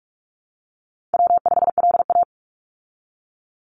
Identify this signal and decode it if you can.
Morse code, heard as a keyed tone, about 44 wpm, 700 Hz. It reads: W5LA
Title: Call Signs Repeated 2x - 45wpm